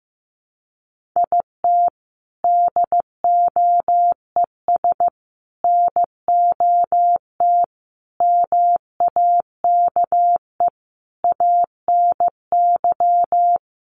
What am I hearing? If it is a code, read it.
Morse code, heard as a keyed tone, about 15 wpm, 700 Hz. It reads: IT DOES NOT MAKE ANY
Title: Sets of 5 Words Encore2 15wpm